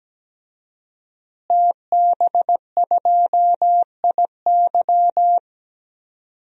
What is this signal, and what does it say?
Morse code, heard as a keyed tone, about 17 wpm, 700 Hz. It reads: TB2IY
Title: DXCC-100 Call Signs - 5 Characters - Repeated 2x - 17wpm